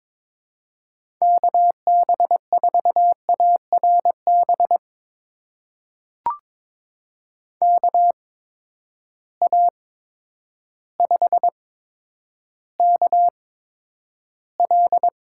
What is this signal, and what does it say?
Morse code, heard as a keyed tone, about 22 wpm, spaced as for 10 wpm, 700 Hz. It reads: KB4ARB KA5KL
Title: USA Call Signs - 6 Characters - 8x Character Spacing - 22wpm